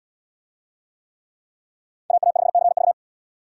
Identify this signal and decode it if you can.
Morse code, heard as a keyed tone, about 77 wpm, 700 Hz. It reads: AA5CV